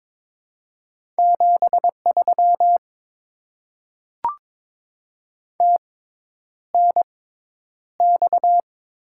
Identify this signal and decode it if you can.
Morse code, heard as a keyed tone, about 22 wpm, spaced as for 10 wpm, 700 Hz. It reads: 73 TNX
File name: Top 20 QSO Elements - 6x Character Spacing - 22wpm